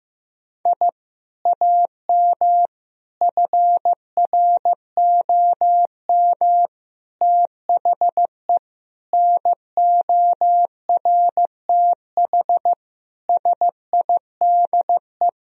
Morse code, heard as a keyed tone, about 15 wpm, 700 Hz. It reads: I AM FROM THE NORTH SIDE
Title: Sentences from Top 1000 Words - Review Part 1 - 15wpm